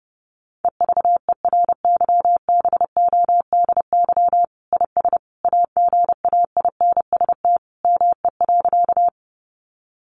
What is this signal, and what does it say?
Morse code, heard as a keyed tone, about 30 wpm, 700 Hz. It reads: EVERYBODY IS AGAINST ME.
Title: Sentences from Top 1400 Words 30wpm